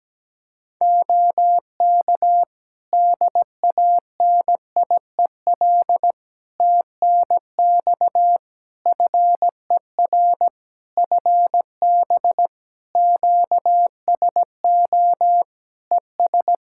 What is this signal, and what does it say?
Morse code, heard as a keyed tone, about 17 wpm, 700 Hz. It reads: OK DANIEL TNX FER FB QSO ES
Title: Practice QSO - Exchange 4 - Ending - 17wpm